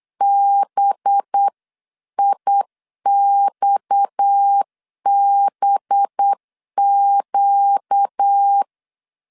Morse code, heard as a keyed tone, about 8 wpm, 800 Hz. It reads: BIXBQ
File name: LEZIONI / 6 PKX0 / 6 PKX 8wpm 3041